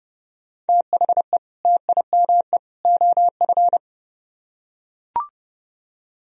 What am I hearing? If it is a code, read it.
Morse code, heard as a keyed tone, about 30 wpm, 700 Hz. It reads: THE TIME OF